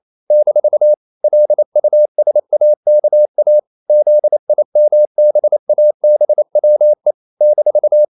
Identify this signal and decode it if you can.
Morse code, heard as a keyed tone, about 28 wpm, 600 Hz. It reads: - LUSAKA ZIMBABWE -
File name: CW / Practice / NationalCapitals28